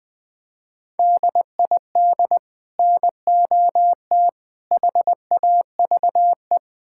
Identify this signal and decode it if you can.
Morse code, heard as a keyed tone, about 21 wpm, 700 Hz. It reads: DID NOT HAVE
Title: Build A Phrase - 4-Word Phrases - 20wpm